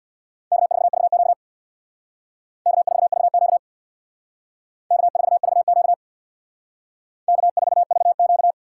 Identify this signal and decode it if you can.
Morse code, heard as a keyed tone, about 64 wpm, 690 Hz. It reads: K4VQ K4VQ K4VQ K4VQ